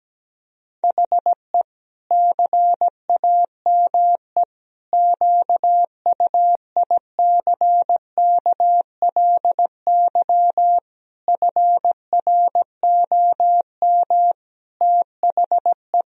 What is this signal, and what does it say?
Morse code, heard as a keyed tone, about 17 wpm, 700 Hz. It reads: HE CAME QUICKLY FROM THE
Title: Sentences from Top 1000 Words - Review Part 1 - 17wpm